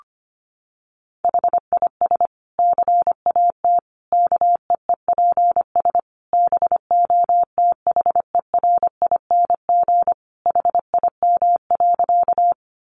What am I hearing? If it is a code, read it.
Morse code, heard as a keyed tone, about 25 wpm, 700 Hz. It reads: HIS CAT KEEPS BOTHERING HIM.
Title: Additional Easy Phrases 25wpm